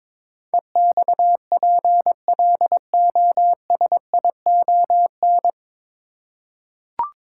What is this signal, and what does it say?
Morse code, heard as a keyed tone, about 22 wpm, 700 Hz. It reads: EXPLOSION
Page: Morse Code Ninja practice 22wpm